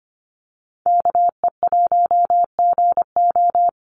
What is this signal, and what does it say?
Morse code, heard as a keyed tone, about 25 wpm, 700 Hz. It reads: KE1GO
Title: USA Call Signs 25wpm